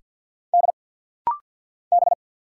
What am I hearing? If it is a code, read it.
Morse code, heard as a keyed tone, about 50 wpm, 700 Hz. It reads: D B
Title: Single Letters - Rapid-Fire 50wpm